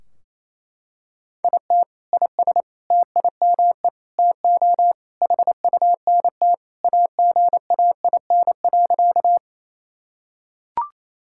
Morse code, heard as a keyed tone, about 28 wpm, 700 Hz. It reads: IT IS TIME TO HUNT AGAIN.
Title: Sentences from Top 1600 Words 28wpm